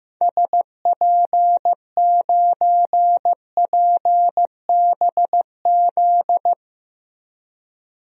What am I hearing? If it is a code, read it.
Morse code, heard as a keyed tone, about 15 wpm, 700 Hz. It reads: SP9PBZ
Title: DXCC-100 Call Signs - 6 Characters - Repeated 2x - 15wpm